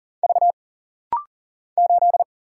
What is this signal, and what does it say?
Morse code, heard as a keyed tone, about 40 wpm, 700 Hz. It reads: V 8
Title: Single Letter-Number - Rapid-Fire 40wpm